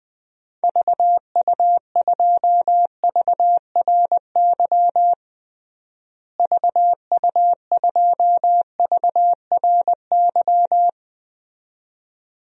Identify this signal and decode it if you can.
Morse code, heard as a keyed tone, about 20 wpm, 700 Hz. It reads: VU2VRY VU2VRY
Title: DXCC-100 Call Signs - 6 Characters - Repeated 2x - 20wpm